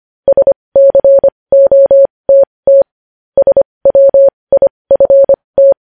Morse code, heard as a keyed tone, about 25 wpm, spaced as for 20 wpm, 550 Hz. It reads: SCOTT SWIFT